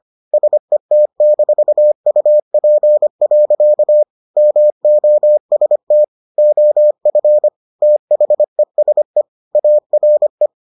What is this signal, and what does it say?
Morse code, heard as a keyed tone, about 25 wpm, 600 Hz. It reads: SET-UP. MOST OF THESE ARE